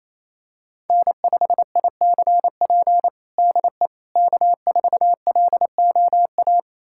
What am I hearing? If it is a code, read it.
Morse code, heard as a keyed tone, about 28 wpm, 700 Hz. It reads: N5ICP DE K4LOA
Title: USA Callsign DE Callsign - 5 Characters - 28wpm